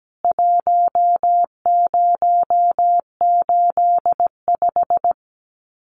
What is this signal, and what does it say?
Morse code, heard as a keyed tone, about 17 wpm, 700 Hz. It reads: 1085
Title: CWT Exchange 17wpm